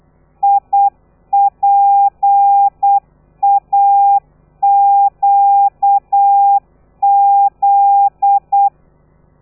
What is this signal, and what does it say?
Morse code, heard as a keyed tone, about 8 wpm, 800 Hz. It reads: IPAQZ
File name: telegrafia / i2tfj / Lez13 35